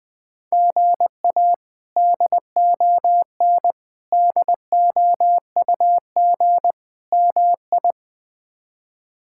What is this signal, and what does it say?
Morse code, heard as a keyed tone, about 20 wpm, 700 Hz. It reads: GA DON DOUG MI
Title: K1USN SST QSO Practice - 20wpm